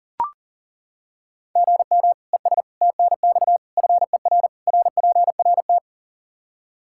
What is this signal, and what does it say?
Morse code, heard as a keyed tone, about 37 wpm, 700 Hz. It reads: GM ES TNX FER RPRT
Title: K1USN SST QSO Practice 40wpm